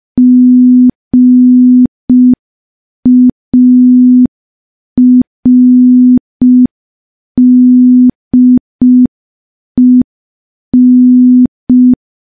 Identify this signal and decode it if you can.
Morse code, heard as a keyed tone, about 5 wpm, 250 Hz. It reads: GARDEN
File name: morse / morse-training-05WPM